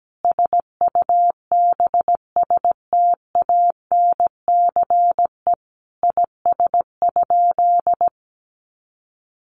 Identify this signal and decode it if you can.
Morse code, heard as a keyed tone, about 17 wpm, 700 Hz. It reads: SUBSTANCE IS?
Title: Sentences from Top 1500 Words 17wpm